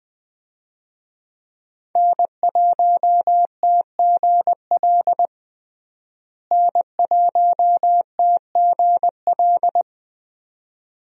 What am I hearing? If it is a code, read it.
Morse code, heard as a keyed tone, about 20 wpm, 700 Hz. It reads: N1TGL N1TGL
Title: USA Call Signs - 5 Characters - Repeated 2x - 20wpm